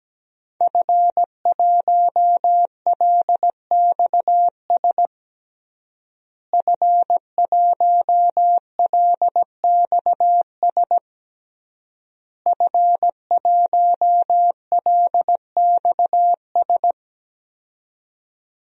Morse code, heard as a keyed tone, about 17 wpm, 700 Hz. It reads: F1LXS F1LXS F1LXS
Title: DXCC-100 Call Signs - 5 Characters - Repeated 3x - 17wpm